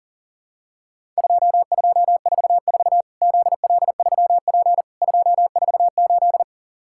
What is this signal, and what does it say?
Morse code, heard as a keyed tone, about 37 wpm, 700 Hz. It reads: 2244 ZL3P 248
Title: Worked All Europe DX Contest - QTC Exchange - 40wpm